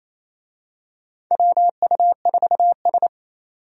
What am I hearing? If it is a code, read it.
Morse code, heard as a keyed tone, about 27 wpm, 700 Hz. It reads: WU4S
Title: USA Call Signs - 4 Characters - 28wpm